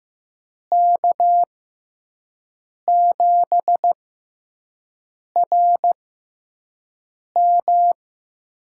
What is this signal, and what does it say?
Morse code, heard as a keyed tone, about 15 wpm, spaced as for 5 wpm, 700 Hz. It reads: K7RM
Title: USA Call Signs - 5 Characters - 6x Character Spacing - 15wpm